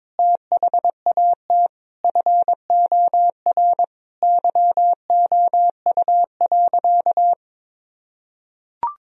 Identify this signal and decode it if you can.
Morse code, heard as a keyed tone, about 22 wpm, 700 Hz. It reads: THAT FOR YOU.
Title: Sentences from Top 1000 Words - Review Part 3 - 22wpm